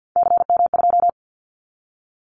Morse code, heard as a keyed tone, about 50 wpm, 700 Hz. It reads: CM2
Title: Callsign Prefixes for Top 100 DXCC - 50wpm